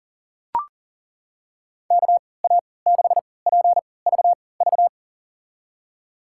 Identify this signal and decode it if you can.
Morse code, heard as a keyed tone, about 40 wpm, spaced as for 23 wpm, 700 Hz. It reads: KA6PVV
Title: USA Call Signs - 6 Characters - 3x Character Spacing - 40wpm